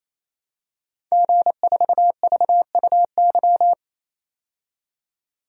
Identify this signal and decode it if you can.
Morse code, heard as a keyed tone, about 28 wpm, 700 Hz. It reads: G4VUY
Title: DXCC-100 Call Signs - 5 Characters - 28wpm